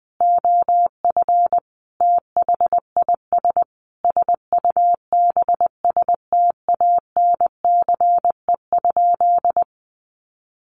Morse code, heard as a keyed tone, about 19 wpm, 700 Hz. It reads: OF THIS SUBSTANCE?
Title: Sentences from Top 1500 Words 20wpm